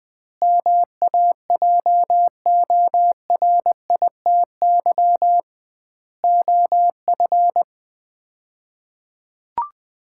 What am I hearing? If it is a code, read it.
Morse code, heard as a keyed tone, about 20 wpm, 700 Hz. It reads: MAJORITY OF
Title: 2x Word-Spacing - Sets of 4 Words - 20wpm